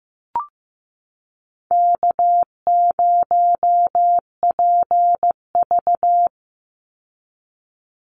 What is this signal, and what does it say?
Morse code, heard as a keyed tone, about 15 wpm, 700 Hz. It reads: K0PV
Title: Easy Call Signs - 15wpm